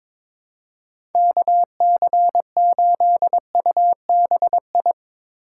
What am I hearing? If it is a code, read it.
Morse code, heard as a keyed tone, about 22 wpm, 700 Hz. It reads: KC8UBI
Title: USA Call Signs - 2x Character Spacing - 22wpm